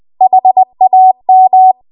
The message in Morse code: HAM